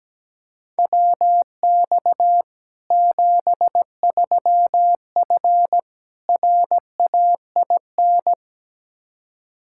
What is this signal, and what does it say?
Morse code, heard as a keyed tone, about 17 wpm, 700 Hz. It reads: WX 73F RAIN
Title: Practice QSO - Exchange 2 - Info - 17wpm